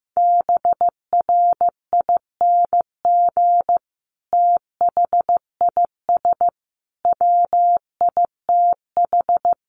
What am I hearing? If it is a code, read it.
Morse code, heard as a keyed tone, about 15 wpm, 700 Hz. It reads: BRING THIS WITH
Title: Sentences from Top 500 Words 15wpm